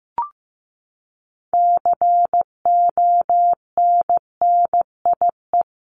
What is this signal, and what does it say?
Morse code, heard as a keyed tone, about 15 wpm, 700 Hz. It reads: CONNIE
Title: CWT Exchange 15wpm